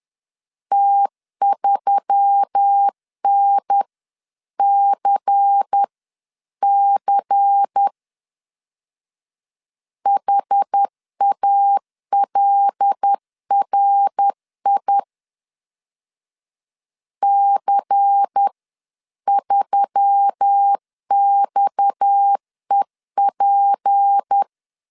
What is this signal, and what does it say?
Morse code, heard as a keyed tone, about 10 wpm, spaced as for 9 wpm, 800 Hz. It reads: T3NCC HALRI C3XEP